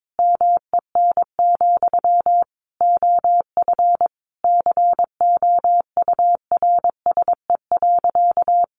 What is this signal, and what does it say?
Morse code, heard as a keyed tone, about 22 wpm, 700 Hz. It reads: MEN, OF COURSE.
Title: Sentences from Top 1100 Words 22wpm